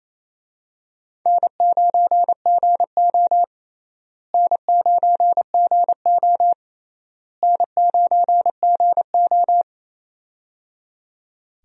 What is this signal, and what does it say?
Morse code, heard as a keyed tone, about 28 wpm, 700 Hz. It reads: N9GO N9GO N9GO
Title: USA Call Signs - 4 Characters - Repeated 3x - 28wpm